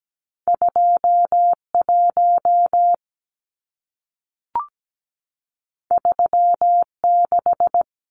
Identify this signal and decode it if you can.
Morse code, heard as a keyed tone, about 17 wpm, 700 Hz. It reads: 21 36
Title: Numbers - 2-Digit 17wpm